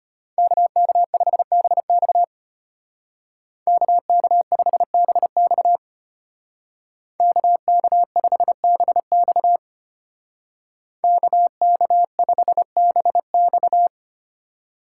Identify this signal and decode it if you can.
Morse code, heard as a keyed tone, about 33 wpm, 700 Hz. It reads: KK5BX KK5BX KK5BX KK5BX